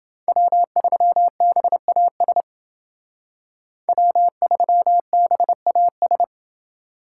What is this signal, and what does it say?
Morse code, heard as a keyed tone, about 29 wpm, 700 Hz. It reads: W3BAS W3BAS